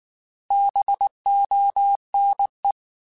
Morse code, heard as a keyed tone, about 19 wpm, 800 Hz. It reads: BONE
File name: Alphabetical / Alphabetical mp3s / Bs @18